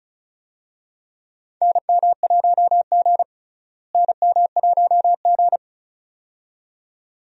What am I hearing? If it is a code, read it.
Morse code, heard as a keyed tone, about 36 wpm, 700 Hz. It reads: NM1G NM1G
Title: USA Call Signs - 4 Characters - Repeated 2x - 35wpm